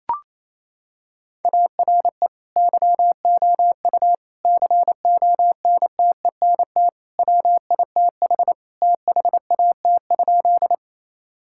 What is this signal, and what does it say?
Morse code, heard as a keyed tone, about 28 wpm, 700 Hz. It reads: ARE YOU CONTENT WITH THAT?
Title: Sentences from Top 1000 Words - Review Part 6 - 28wpm